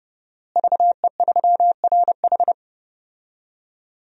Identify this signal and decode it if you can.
Morse code, heard as a keyed tone, about 32 wpm, 700 Hz. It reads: VE3RH
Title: DXCC-100 Call Signs - 5 Characters - Repeated 2x - 30wpm